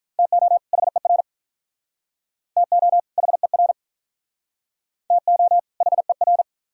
Morse code, heard as a keyed tone, about 46 wpm, 700 Hz. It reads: TO HER TO HER TO HER